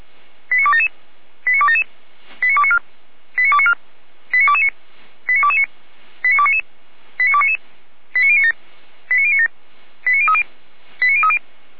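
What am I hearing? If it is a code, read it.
Five-tone selective call (ZVEI), heard as quick runs of five tones, tones of about 70 ms. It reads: 89277 89277 89294 89294 89229 89229 89299 89299 89097 89097 89030 89030
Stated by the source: ZVEI five-tone selective call received by SDR.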